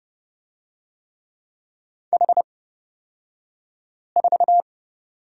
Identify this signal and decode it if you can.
Morse code, heard as a keyed tone, about 31 wpm, spaced as for 5 wpm, 700 Hz. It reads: H4
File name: Lesson 36d - BT - Two Characters 30wpm